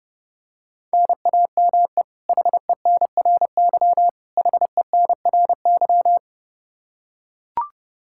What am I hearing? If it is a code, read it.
Morse code, heard as a keyed tone, about 32 wpm, 700 Hz. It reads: NAME HENRY HENRY
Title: Practice QSO - Exchange 1 - Beginning - 30wpm